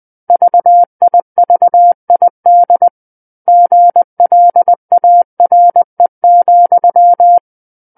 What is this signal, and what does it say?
Morse code, heard as a keyed tone, about 20 wpm, 700 Hz. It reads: VIVID GLARE,
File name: 20 WPM